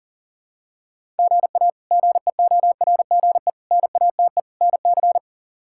Morse code, heard as a keyed tone, about 37 wpm, 700 Hz. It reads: GA GEORGE NATE NC
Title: K1USN SST QSO Practice - 40wpm